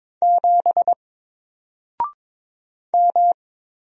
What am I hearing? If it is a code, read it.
Morse code, heard as a keyed tone, about 22 wpm, 700 Hz. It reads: 7 M
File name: Single-Letter-Number-22wpm